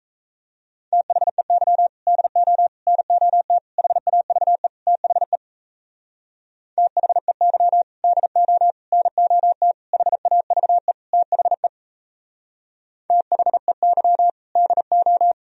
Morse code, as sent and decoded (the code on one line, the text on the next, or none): - .... -.-- / -.. --- / -. --- - / .... .- ...- . / - .... / - .... -.-- / -.. --- / -. --- - / .... .- ...- . / - .... / - .... -.-- / -.. ---
THEY DO NOT HAVE THE THEY DO NOT HAVE THE THEY DO